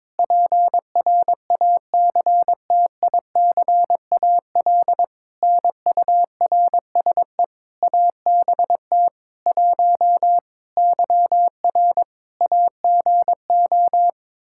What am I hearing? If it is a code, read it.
Morse code, heard as a keyed tone, about 22 wpm, 700 Hz. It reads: PRACTICAL NURSE ABT 1 YR AGO
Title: Practice QSO - Exchange 3 - More Info - 22wpm